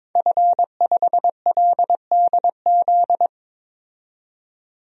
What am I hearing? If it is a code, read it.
Morse code, heard as a keyed tone, about 21 wpm, 700 Hz. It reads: F5LDZ
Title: DXCC-100 Call Signs - 5 Characters - 22wpm